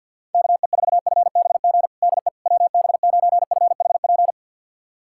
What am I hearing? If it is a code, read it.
Morse code, heard as a keyed tone, about 51 wpm, 700 Hz. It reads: KE4FBG DE WB9FHP
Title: USA Callsign DE Callsign - 6 Characters - 50wpm